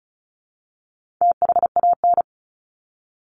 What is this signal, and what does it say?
Morse code, heard as a keyed tone, about 34 wpm, 700 Hz. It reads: THAN
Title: Top 300 Words Repeated 35wpm